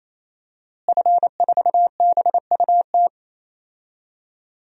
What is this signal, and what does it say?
Morse code, heard as a keyed tone, about 27 wpm, 700 Hz. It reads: F4BUT
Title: DXCC-100 Call Signs - 5 Characters - Repeated 3x - 28wpm